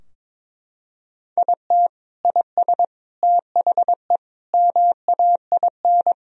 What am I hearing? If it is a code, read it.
Morse code, heard as a keyed tone, about 22 wpm, 700 Hz. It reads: IT IS THE MAIN
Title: Sentences from Top 1000 Words - Review Part 2 - 22wpm